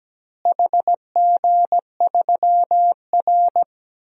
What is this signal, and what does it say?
Morse code, heard as a keyed tone, about 17 wpm, 700 Hz. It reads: HG3R
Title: DXCC-100 Call Signs - 4 Characters - Repeated 2x - 17wpm